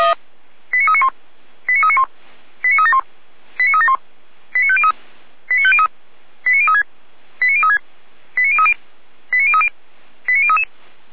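Five-tone selective call (ZVEI), heard as quick runs of five tones, tones of about 80 ms. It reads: 89391 89391 89371 89371 89503 89503 89036 89036 89030 89030 89033
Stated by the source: ZVEI five-tone call received by SDR.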